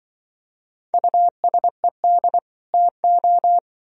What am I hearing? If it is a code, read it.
Morse code, heard as a keyed tone, about 24 wpm, 700 Hz. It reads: USED TO
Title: Speed-Racing - Sets of 2 Words 20wpm